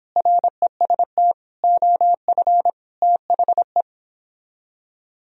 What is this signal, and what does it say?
Morse code, heard as a keyed tone, about 31 wpm, 700 Hz. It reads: REST OF THE